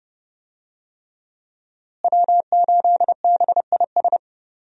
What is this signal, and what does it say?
Morse code, heard as a keyed tone, about 30 wpm, 700 Hz. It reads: W8BIS
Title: USA Call Signs - 5 Characters - Repeated 3x - 30wpm